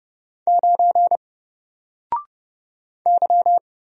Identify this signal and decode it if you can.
Morse code, heard as a keyed tone, about 30 wpm, 700 Hz. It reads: 9 Y
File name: Single-Letter-Number-30wpm